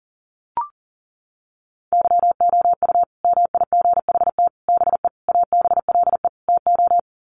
Morse code, heard as a keyed tone, about 40 wpm, 700 Hz. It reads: YOU MIGHT BE ABLE TO
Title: Sets of 5 Words Encore2 40wpm